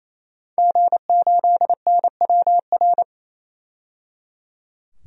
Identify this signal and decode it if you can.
Morse code, heard as a keyed tone, about 27 wpm, 700 Hz. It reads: G8NWR
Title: DXCC-100 Call Signs - 5 Characters - 28wpm